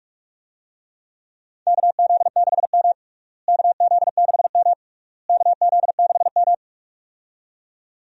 Morse code, heard as a keyed tone, about 47 wpm, 700 Hz. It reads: KZ6M KZ6M KZ6M
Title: USA Call Signs - 4 Characters - Repeated 3x - 45wpm